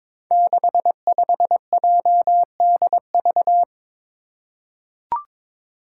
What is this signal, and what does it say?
Morse code, heard as a keyed tone, about 22 wpm, 700 Hz. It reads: 65JDV
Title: Five Challenging Letter-Number Combinations 22wpm